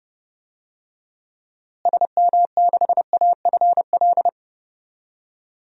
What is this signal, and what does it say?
Morse code, heard as a keyed tone, about 30 wpm, 700 Hz. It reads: SM6AFL
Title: DXCC-100 Call Signs - 6 Characters - Repeated 3x - 30wpm